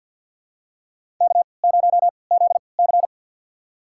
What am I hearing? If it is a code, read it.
Morse code, heard as a keyed tone, about 47 wpm, spaced as for 28 wpm, 700 Hz. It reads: K0ZC
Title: USA Call Signs - 4 Characters - 3x Character Spacing - 50wpm